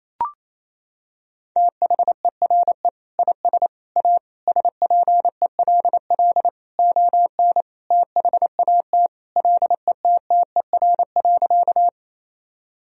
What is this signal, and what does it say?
Morse code, heard as a keyed tone, about 28 wpm, 700 Hz. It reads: THERE IS A SPELL ON THAT LETTER.
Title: Sentences from Top 1000 Words - Review Part 1 - 28wpm